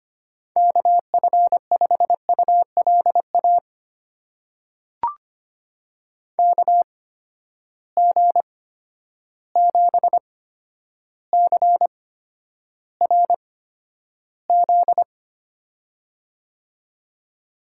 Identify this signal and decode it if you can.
Morse code, heard as a keyed tone, about 25 wpm, spaced as for 11 wpm, 700 Hz. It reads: KF5ULA KG7CRZ
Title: USA Call Signs - 6 Characters - 8x Character Spacing - 25wpm